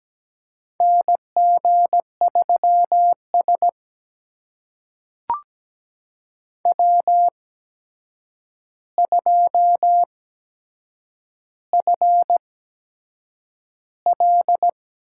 Morse code, heard as a keyed tone, about 17 wpm, spaced as for 7 wpm, 700 Hz. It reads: NG3S W2FL
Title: USA Call Signs - 4 Characters - 8x Character Spacing - 17wpm